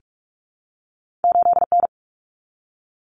Morse code, heard as a keyed tone, about 46 wpm, 700 Hz. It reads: 8N
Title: Common Cut Numbers - 2-Digit 45wpm